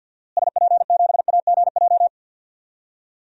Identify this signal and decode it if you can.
Morse code, heard as a keyed tone, about 52 wpm, 700 Hz. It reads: SP7AGJ
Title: DXCC-100 Call Signs - 6 Characters - Repeated 2x - 50wpm